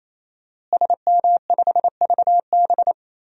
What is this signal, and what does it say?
Morse code, heard as a keyed tone, about 28 wpm, 700 Hz. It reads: SM5VB
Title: DXCC-100 Call Signs - 5 Characters - Repeated 3x - 28wpm